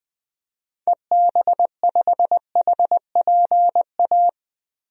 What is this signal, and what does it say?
Morse code, heard as a keyed tone, about 20 wpm, 700 Hz. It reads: EB5HPA
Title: DXCC-100 Call Signs - 6 Characters - Repeated 3x - 20wpm